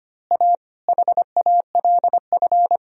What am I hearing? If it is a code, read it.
Morse code, heard as a keyed tone, about 24 wpm, 700 Hz. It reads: A HALF
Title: Speed-Racing - Sets of 5 Words 25wpm